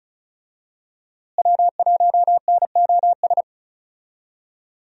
Morse code, heard as a keyed tone, about 35 wpm, 700 Hz. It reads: W1NOS
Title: USA Call Signs - 5 Characters - Repeated 2x - 35wpm